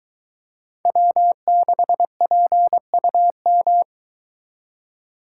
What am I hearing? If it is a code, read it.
Morse code, heard as a keyed tone, about 23 wpm, 700 Hz. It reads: W6PUM